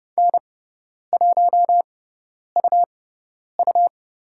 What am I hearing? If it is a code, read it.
Morse code, heard as a keyed tone, about 30 wpm, spaced as for 10 wpm, 700 Hz. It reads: N1UU
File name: Lesson 18g - Letter m - Call Signs 30wpm